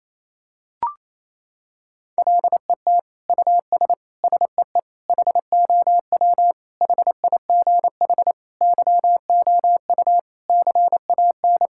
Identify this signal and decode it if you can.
Morse code, heard as a keyed tone, about 28 wpm, 700 Hz. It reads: LET US SEE HOW HIGH YOU CAN
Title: Sentences from Top 1000 Words - Review Part 6 - 28wpm